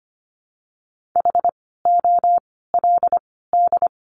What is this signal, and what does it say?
Morse code, heard as a keyed tone, about 25 wpm, 700 Hz. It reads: H O L D
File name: course / 25:10wpm / 094 Lesson 16f - Letter c - Words 25wpm